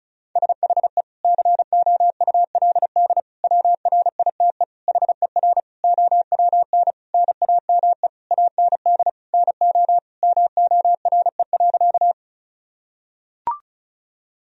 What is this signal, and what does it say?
Morse code, heard as a keyed tone, about 36 wpm, 700 Hz. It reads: SHE COULD WRITE HER OWN NAME AND NO MORE.
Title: Sentences from Top 1000 Words - Review Part 1 - 35wpm